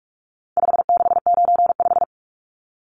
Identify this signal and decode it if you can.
Morse code, heard as a keyed tone, about 44 wpm, 700 Hz. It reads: 5695
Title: Numbers - 4-Digit 45wpm